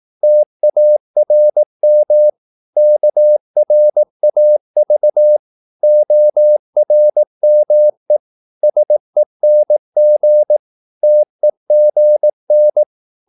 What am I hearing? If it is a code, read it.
Morse code, heard as a keyed tone, about 18 wpm, 600 Hz. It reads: TARM KRAV ORME SENG TEGN
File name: Korte ord DK 18wpm